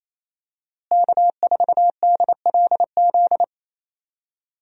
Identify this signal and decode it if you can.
Morse code, heard as a keyed tone, about 28 wpm, 700 Hz. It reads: K4DLZ
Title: USA Call Signs - 5 Characters - Repeated 3x - 28wpm